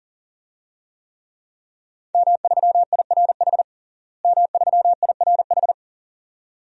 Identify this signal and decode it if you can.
Morse code, heard as a keyed tone, about 40 wpm, 700 Hz. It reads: M3IRH M3IRH
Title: DXCC-100 Call Signs - 5 Characters - Repeated 2x - 40wpm